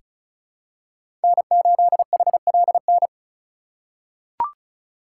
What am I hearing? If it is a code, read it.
Morse code, heard as a keyed tone, about 36 wpm, 700 Hz. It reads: N8HLN
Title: USA Call Signs - 5 Characters - 6x Character Spacing - 35wpm